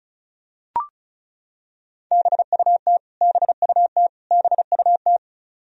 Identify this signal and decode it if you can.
Morse code, heard as a keyed tone, about 36 wpm, 700 Hz. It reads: BUT BUT BUT
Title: Top 100 Words Repeated 35wpm